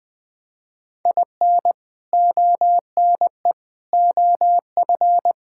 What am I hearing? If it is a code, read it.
Morse code, heard as a keyed tone, about 20 wpm, 700 Hz. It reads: IN ONE OF
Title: Build A Phrase - 3-Word Phrases - 20wpm